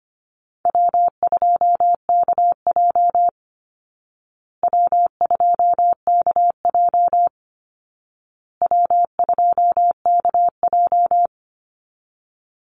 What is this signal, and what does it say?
Morse code, heard as a keyed tone, about 25 wpm, 700 Hz. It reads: W2KJ W2KJ W2KJ
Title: Easy Call Signs Repeated 3x - 25wpm